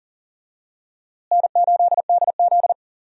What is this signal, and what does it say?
Morse code, heard as a keyed tone, about 40 wpm, 700 Hz. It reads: N8DZ
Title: USA Call Signs - 4 Characters - Repeated 3x - 40wpm